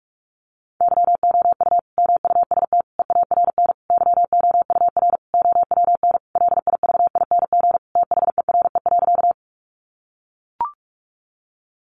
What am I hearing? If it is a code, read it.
Morse code, heard as a keyed tone, about 44 wpm, 700 Hz. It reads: YOU MUST EARN YOUR OWN LIVING THERE.
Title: Sentences from Top 1500 Words 45wpm